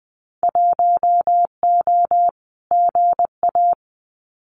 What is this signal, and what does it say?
Morse code, heard as a keyed tone, about 20 wpm, 700 Hz. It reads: 1O GA
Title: Winter Field Day Exchanges 20wpm